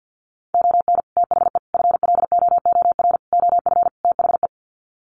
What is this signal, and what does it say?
Morse code, heard as a keyed tone, about 50 wpm, 700 Hz. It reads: ON THE FLOOR OF THE
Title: Sets of 5 Words 50wpm